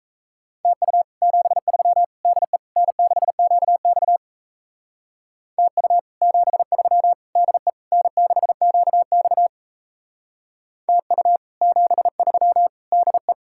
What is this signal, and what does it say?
Morse code, heard as a keyed tone, about 37 wpm, 700 Hz. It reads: TU 73 DE N6QX TU 73 DE N6QX TU 73 DE